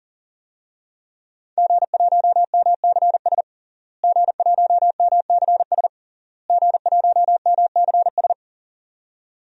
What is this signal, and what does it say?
Morse code, heard as a keyed tone, about 37 wpm, 700 Hz. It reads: G1MCS G1MCS G1MCS
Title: DXCC-100 Call Signs - 5 Characters - Repeated 3x - 40wpm